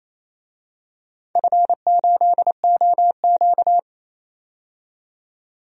Morse code, heard as a keyed tone, about 28 wpm, 700 Hz. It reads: F8OQ
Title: DXCC-100 Call Signs - 4 Characters - 28wpm